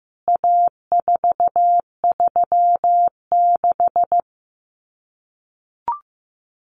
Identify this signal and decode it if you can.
Morse code, heard as a keyed tone, about 15 wpm, 700 Hz. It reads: A436
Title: CWT Exchange 15wpm